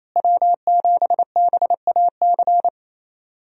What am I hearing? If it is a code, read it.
Morse code, heard as a keyed tone, about 28 wpm, 700 Hz. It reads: W7BAC
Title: USA Call Signs - 5 Characters - Repeated 3x - 28wpm